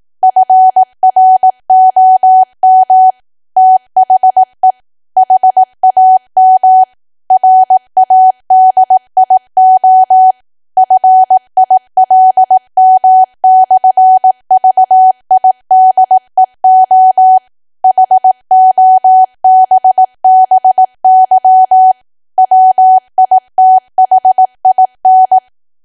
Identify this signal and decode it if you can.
Morse code, heard as a keyed tone, about 18 wpm, 750 Hz. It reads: FROM THE HAM RADIO FILM/VIDEO HOBBY WITHIN
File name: Archive / 18 WPM / 18WPM